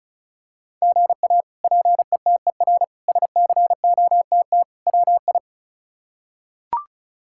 Morse code, heard as a keyed tone, about 36 wpm, 700 Hz. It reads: GA PETER SCOTT WI
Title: K1USN SST QSO Practice - 35wpm